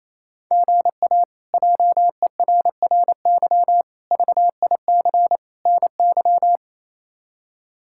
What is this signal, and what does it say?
Morse code, heard as a keyed tone, about 27 wpm, 700 Hz. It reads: GA JERRY VIC NY
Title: K1USN SST QSO Practice - 28wpm